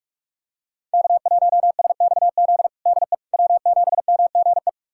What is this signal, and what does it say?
Morse code, heard as a keyed tone, about 42 wpm, 700 Hz. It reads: K1SXZ DE W7MGE
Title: USA Callsign DE Callsign - 5 Characters - 45wpm